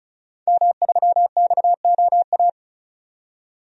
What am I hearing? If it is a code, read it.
Morse code, heard as a keyed tone, about 34 wpm, 700 Hz. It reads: M3XOA
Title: DXCC-100 Call Signs - 5 Characters - 35wpm